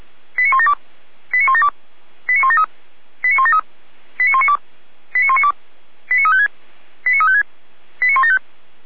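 Five-tone selective call (ZVEI), heard as quick runs of five tones, tones of about 70 ms. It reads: 89172 89172 89173 89173 89192 89192 89356 89356 89175
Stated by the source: ZVEI five-tone selective call received on SDR.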